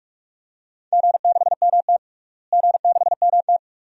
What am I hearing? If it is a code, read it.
Morse code, heard as a keyed tone, about 43 wpm, 700 Hz. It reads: G6MT G6MT
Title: DXCC-100 Call Signs - 4 Characters - Repeated 2x - 45wpm